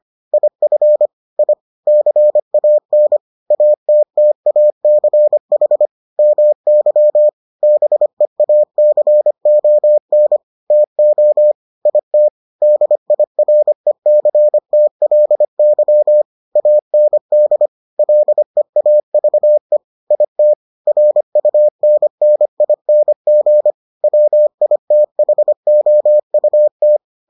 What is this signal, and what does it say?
Morse code, heard as a keyed tone, about 25 wpm, 600 Hz. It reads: IF I CAN ATTACH MY BEACON TO IT DIRECTLY AND LEAVE IT RUNNING WITHOUT